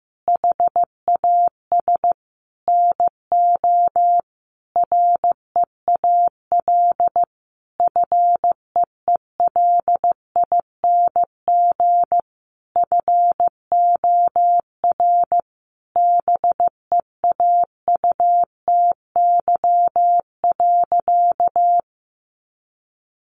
Morse code, heard as a keyed tone, about 15 wpm, 700 Hz. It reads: HAS NO REAL FEELING FOR BEAUTY.
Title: Sentences from Top 1200 Words 15wpm